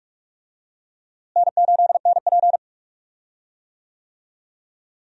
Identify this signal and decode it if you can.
Morse code, heard as a keyed tone, about 46 wpm, 700 Hz. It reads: N8NP
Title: USA Call Signs - 4 Characters - Repeated 3x - 45wpm